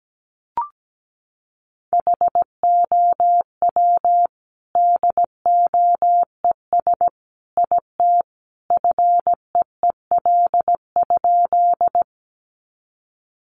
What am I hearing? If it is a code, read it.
Morse code, heard as a keyed tone, about 17 wpm, 700 Hz. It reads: HOW DOES IT FEEL?
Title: Sentences from Top 400 Words 17wpm